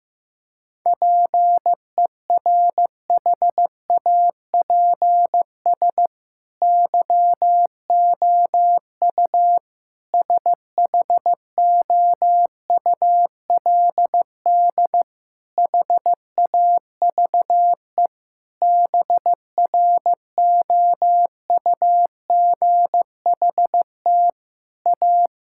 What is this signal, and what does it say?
Morse code, heard as a keyed tone, about 16 wpm, 700 Hz. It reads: PERHAPS YOU SHOULD HAVE BROUGHT A
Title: Sentences from Top 1000 Words - Review Part 6 - 15wpm